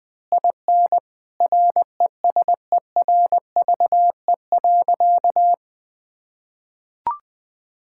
Morse code, heard as a keyed tone, about 21 wpm, 700 Hz. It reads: IN RESERVE.
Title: Sentences from Top 1000 Words - Review Part 6 - 20wpm